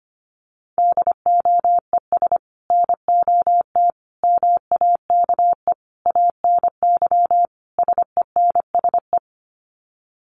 Morse code, heard as a keyed tone, about 25 wpm, 700 Hz. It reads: DOES NOT MAKE ANY SENSE
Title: Sets of 5 Words Encore2 25wpm